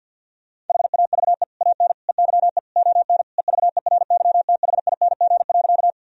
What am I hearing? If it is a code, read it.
Morse code, heard as a keyed tone, about 51 wpm, 700 Hz. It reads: HAVE AN EYE ON EVERYTHING.
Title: Sentences from Top 1000 Words - Review Part 1 - 50wpm